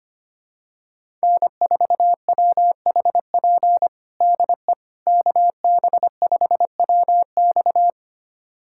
Morse code, heard as a keyed tone, about 25 wpm, 700 Hz. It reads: N4WHP DE KB5WX
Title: USA Callsign DE Callsign - 5 Characters - 25wpm